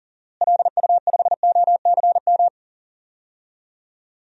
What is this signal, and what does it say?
Morse code, heard as a keyed tone, about 40 wpm, 700 Hz. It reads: LU5OCM
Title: DXCC-100 Call Signs - 6 Characters - Repeated 2x - 40wpm